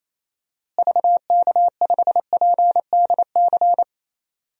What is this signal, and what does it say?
Morse code, heard as a keyed tone, about 28 wpm, 700 Hz. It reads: VK5PDC
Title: DXCC-100 Call Signs - 6 Characters - Repeated 2x - 28wpm